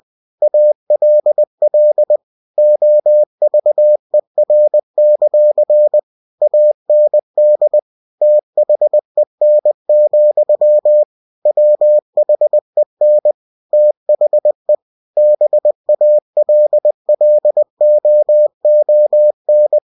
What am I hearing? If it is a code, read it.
Morse code, heard as a keyed tone, about 20 wpm, 600 Hz. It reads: ALL OVER; AND THEN, WHEN THE BALLOON
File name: cw-books / winnie-the-pooh / winnie-20wpm-page-0006